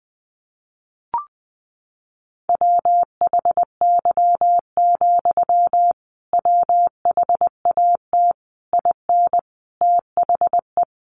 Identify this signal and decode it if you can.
Morse code, heard as a keyed tone, about 20 wpm, 700 Hz. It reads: WHY, WHAT IN THE
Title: Sentences from Top 900 Words 20wpm